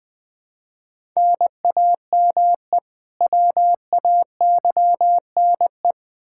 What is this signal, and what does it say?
Morse code, heard as a keyed tone, about 21 wpm, 700 Hz. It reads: NAME WAYNE
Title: Practice QSO - Exchange 1 - Beginning - 20wpm